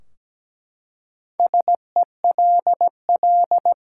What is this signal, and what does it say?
Morse code, heard as a keyed tone, about 17 wpm, 700 Hz. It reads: SELL
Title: Four Letter Words - 17wpm